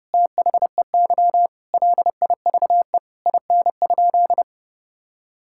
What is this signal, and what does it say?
Morse code, heard as a keyed tone, about 30 wpm, 700 Hz. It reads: THEY LIVE IN?
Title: Sentences from Top 1000 Words - Review Part 5 - 30wpm